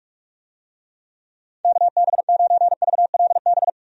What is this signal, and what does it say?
Morse code, heard as a keyed tone, about 46 wpm, 700 Hz. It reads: KB9VLB
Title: USA Call Signs - 6 Characters - 45wpm